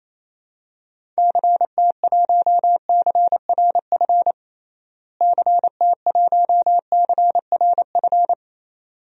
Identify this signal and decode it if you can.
Morse code, heard as a keyed tone, about 28 wpm, 700 Hz. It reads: CT1CRF CT1CRF
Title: DXCC-100 Call Signs - 6 Characters - Repeated 2x - 28wpm